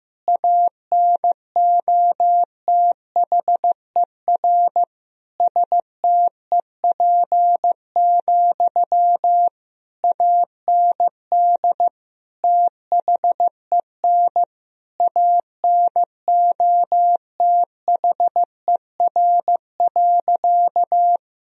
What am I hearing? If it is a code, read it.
Morse code, heard as a keyed tone, about 15 wpm, 700 Hz. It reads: ANOTHER STEP, AND THEN ANOTHER.
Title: Sentences from Top 1000 Words - Review Part 2 - 15wpm